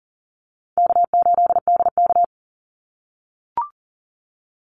Morse code, heard as a keyed tone, about 40 wpm, 700 Hz. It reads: K8DK
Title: Call Signs - 40wpm